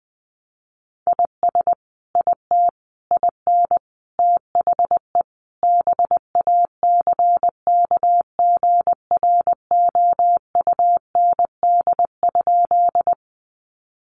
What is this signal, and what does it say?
Morse code, heard as a keyed tone, about 20 wpm, 700 Hz. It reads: IS IT IN THE BACKGROUND?
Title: Sentences from Top 1300 Words 20wpm